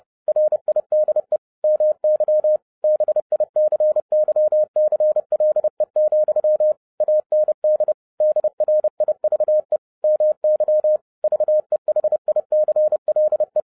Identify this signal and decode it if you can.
Morse code, heard as a keyed tone, about 30 wpm, 610 Hz. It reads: RIDE MY BICYCLE, AND DRIVE MY VEHICLE
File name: 2011-07-mp3 / n60 BACK FROM A BIG ACCIDENT 30wpm